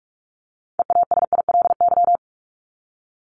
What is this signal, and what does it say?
Morse code, heard as a keyed tone, about 44 wpm, 700 Hz. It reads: EASILY
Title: LY Suffix - Top 100 Words with Base Words 45wpm